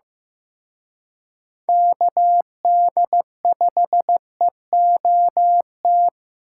Morse code, heard as a keyed tone, about 18 wpm, 700 Hz. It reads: KD5EOT
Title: Speed Racing - USA 2x3 Call Signs 15wpm